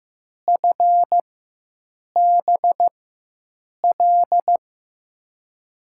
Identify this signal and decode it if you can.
Morse code, heard as a keyed tone, about 16 wpm, spaced as for 7 wpm, 700 Hz. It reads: FBL
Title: USA Call Signs - 6 Characters - 4x Character Spacing - 15wpm